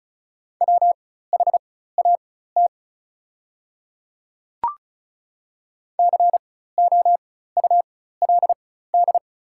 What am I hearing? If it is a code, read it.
Morse code, heard as a keyed tone, about 36 wpm, spaced as for 16 wpm, 690 Hz. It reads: WHAT COULD
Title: 4x Character Spacing - Top 100 Words - 35wpm